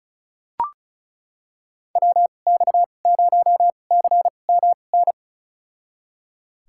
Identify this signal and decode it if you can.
Morse code, heard as a keyed tone, about 34 wpm, spaced as for 25 wpm, 700 Hz. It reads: WX0CMN
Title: USA Call Signs - 6 Characters - 2x Character Spacing - 35wpm